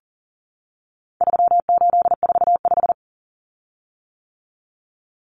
Morse code, heard as a keyed tone, about 40 wpm, 700 Hz. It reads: 3845
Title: Common Cut Numbers - 4-Digit 40wpm